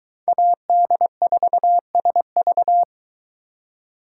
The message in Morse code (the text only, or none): AD4SV